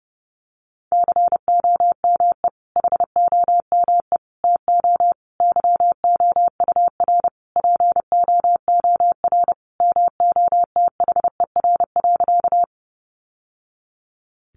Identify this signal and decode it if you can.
Morse code, heard as a keyed tone, about 30 wpm, 700 Hz. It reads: COME HOME TO YOUR POOR MOTHER.
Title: Sentences from Top 1000 Words 30wpm